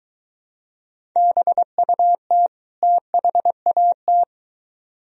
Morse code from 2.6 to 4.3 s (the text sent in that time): THAT